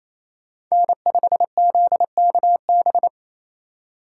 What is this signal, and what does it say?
Morse code, heard as a keyed tone, about 28 wpm, 700 Hz. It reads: N5ZKB
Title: USA Call Signs - 5 Characters - 4x Character Spacing - 28wpm